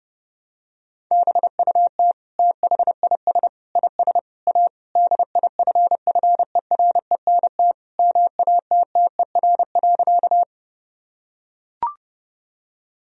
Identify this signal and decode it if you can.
Morse code, heard as a keyed tone, about 30 wpm, 700 Hz. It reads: BUT THIS IS A DIFFERENT MATTER.
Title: Sentences from Top 1000 Words - Review Part 3 - 30wpm